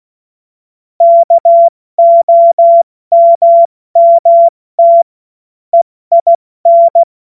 Morse code, heard as a keyed tone, about 16 wpm, 680 Hz. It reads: KOMMT EIN
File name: Morsekurs / Witze / Witz 02 Januar 2024 70 F80